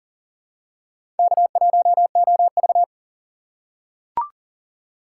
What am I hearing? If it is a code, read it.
Morse code, heard as a keyed tone, about 40 wpm, 700 Hz. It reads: K1OV
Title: USA Call Signs - 4 Characters - 2x Character Spacing - 40wpm